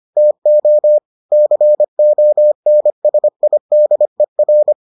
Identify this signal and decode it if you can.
Morse code, heard as a keyed tone, about 25 wpm, 600 Hz. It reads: TO CONSIDER